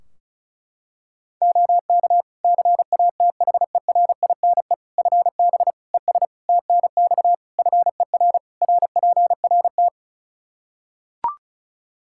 Morse code, heard as a keyed tone, about 36 wpm, 700 Hz. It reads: OK CATHERINE FB ES TNX FER RPRT
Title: Practice QSO - Exchange 2 - Info - 35wpm